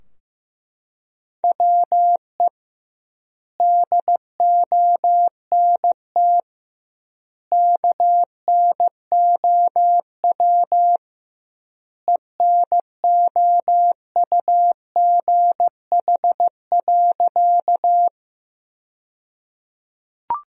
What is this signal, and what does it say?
Morse code, heard as a keyed tone, about 15 wpm, 700 Hz. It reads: WE DONT KNOW ENOUGH.
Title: 2x Word-Spacing - Sentences from Top 400 Words - 15wpm